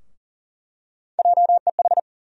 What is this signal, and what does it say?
Morse code, heard as a keyed tone, about 40 wpm, 700 Hz. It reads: JEH
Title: USA Call Sign Postfixes - 3 Letters 40wpm